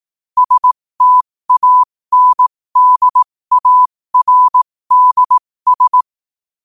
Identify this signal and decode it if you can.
Morse code, heard as a keyed tone, about 18 wpm, spaced as for 15 wpm, 1000 Hz. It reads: STANDARDS
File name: 15wpm